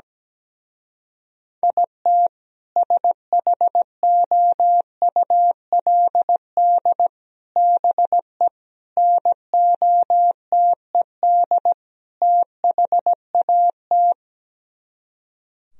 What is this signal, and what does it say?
Morse code, heard as a keyed tone, about 17 wpm, 700 Hz. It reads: IT SHOULD BE NOTED THAT
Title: Speed-Racing - Sets of 5 Words 17wpm